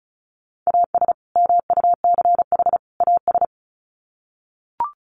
Morse code, heard as a keyed tone, about 34 wpm, 700 Hz. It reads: AS MUCH AS
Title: Sets of 3 Words 35wpm